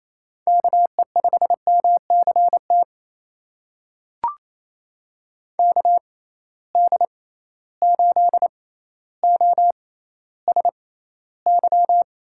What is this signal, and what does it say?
Morse code, heard as a keyed tone, about 28 wpm, spaced as for 14 wpm, 700 Hz. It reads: KE5MCT KD8OSY
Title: USA Call Signs - 6x Character Spacing - 28wpm